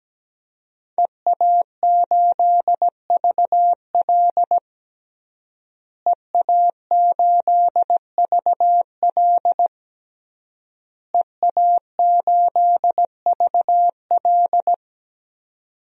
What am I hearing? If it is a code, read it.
Morse code, heard as a keyed tone, about 17 wpm, 700 Hz. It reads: EA8VL EA8VL EA8VL
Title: DXCC-100 Call Signs - 5 Characters - Repeated 3x - 17wpm